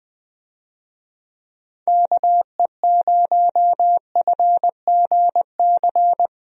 Morse code, heard as a keyed tone, about 20 wpm, 700 Hz. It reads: KE0FGC
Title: USA Call Signs - 6 Characters - Repeated 3x - 20wpm